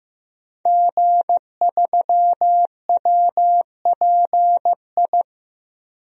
Morse code, heard as a keyed tone, about 16 wpm, 700 Hz. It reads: G3WPI
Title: DXCC-100 Call Signs - 5 Characters - Repeated 2x - 15wpm